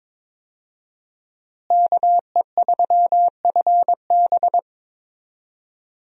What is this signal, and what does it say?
Morse code, heard as a keyed tone, about 22 wpm, 700 Hz. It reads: KE3FB
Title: USA Call Signs - 5 Characters - Repeated 2x - 22wpm